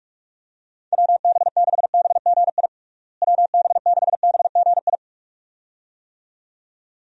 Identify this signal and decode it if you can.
Morse code, heard as a keyed tone, about 46 wpm, 700 Hz. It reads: WB6BGI WB6BGI
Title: USA Call Signs - 6 Characters - Repeated 2x - 45wpm